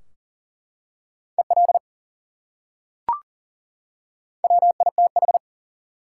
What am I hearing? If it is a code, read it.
Morse code, heard as a keyed tone, about 40 wpm, 700 Hz. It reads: EL WITH
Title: Top 180 QSO Elements 40wpm